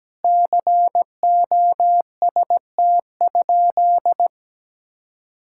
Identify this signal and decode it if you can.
Morse code, heard as a keyed tone, about 17 wpm, 700 Hz. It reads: COST?
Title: Sentences from Top 1000 Words - Review Part 6 - 17wpm